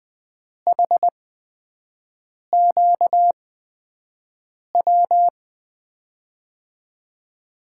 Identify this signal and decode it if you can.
Morse code, heard as a keyed tone, about 20 wpm, spaced as for 5 wpm, 700 Hz. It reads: HQW
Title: USA Call Signs - 8x Character Spacing - 20wpm